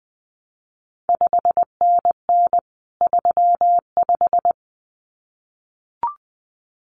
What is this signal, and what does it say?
Morse code, heard as a keyed tone, about 20 wpm, 700 Hz. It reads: 5NN 35
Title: CQ WW DX Contest Exchange - 20wpm